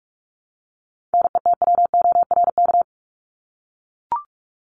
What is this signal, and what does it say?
Morse code, heard as a keyed tone, about 44 wpm, 700 Hz. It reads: NETWORK
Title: Top 751-1000 Words 45wpm